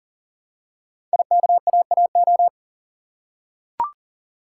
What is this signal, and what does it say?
Morse code, heard as a keyed tone, about 39 wpm, 700 Hz. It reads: IKAAO
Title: Code Groups - 5 Letters - 40wpm